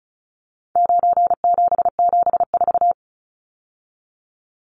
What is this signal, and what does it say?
Morse code, heard as a keyed tone, about 34 wpm, 700 Hz. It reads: 9774
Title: Numbers - 4-Digit 35wpm